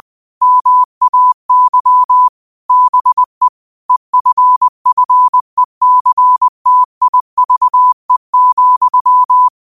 Morse code, heard as a keyed tone, about 20 wpm, 1000 Hz. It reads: MAY BE EFFECTIVE,